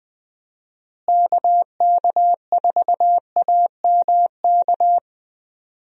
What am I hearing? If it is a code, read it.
Morse code, heard as a keyed tone, about 20 wpm, 700 Hz. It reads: KK4AMK
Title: USA Call Signs - 6 Characters - Repeated 3x - 20wpm